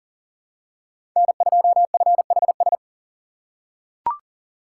Morse code, heard as a keyed tone, about 37 wpm, 700 Hz. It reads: N2FHS
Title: USA Call Signs - 5 Characters - 3x Character Spacing - 40wpm